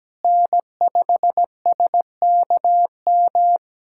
Morse code, heard as a keyed tone, about 17 wpm, 700 Hz. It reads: N5SKM
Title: USA Callsign DE Callsign 17wpm